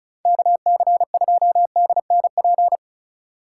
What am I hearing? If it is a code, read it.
Morse code, heard as a keyed tone, about 34 wpm, 700 Hz. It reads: KC2DNP
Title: USA Call Signs - 6 Characters - 4x Character Spacing - 35wpm